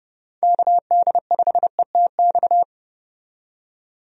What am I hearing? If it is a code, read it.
Morse code, heard as a keyed tone, about 31 wpm, 700 Hz. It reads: KD5ETX
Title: USA Call Signs - 6 Characters - 30wpm